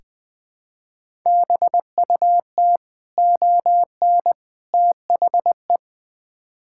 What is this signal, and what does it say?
Morse code, heard as a keyed tone, about 20 wpm, 700 Hz. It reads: BUT ON THE
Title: Build A Phrase - 5-Word Phrases - 20wpm